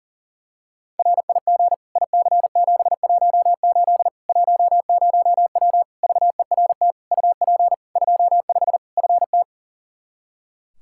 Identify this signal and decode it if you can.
Morse code, heard as a keyed tone, about 37 wpm, 700 Hz. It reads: RIG IC718 10W VERT UP 25 FT
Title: Practice QSO - Exchange 2 - Info - 40wpm